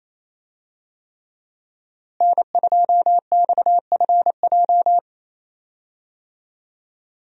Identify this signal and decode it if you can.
Morse code, heard as a keyed tone, about 28 wpm, 700 Hz. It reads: N2XFJ
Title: USA Call Signs - 5 Characters - Repeated 3x - 28wpm